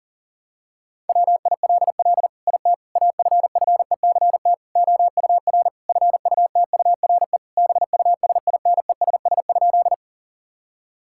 Morse code, heard as a keyed tone, about 40 wpm, 700 Hz. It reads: WILL IT AFFECT OUR FUTURE BUSINESS?
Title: Sentences from Top 1000 Words - Review Part 6 - 40wpm